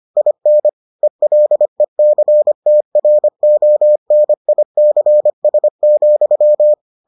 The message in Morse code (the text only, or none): IN ELECTRONICS,